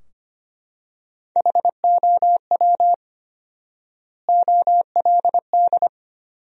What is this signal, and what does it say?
Morse code, heard as a keyed tone, about 25 wpm, 700 Hz. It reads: HOW OLD
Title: 4x Word-Spacing - Sentences from Top 200 Words - 25wpm